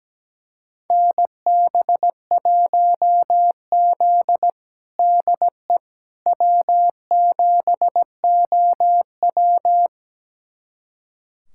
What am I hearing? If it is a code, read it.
Morse code, heard as a keyed tone, about 17 wpm, 700 Hz. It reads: NB1Z DE W7OW
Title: USA Callsign DE Callsign - 4 Characters - 17wpm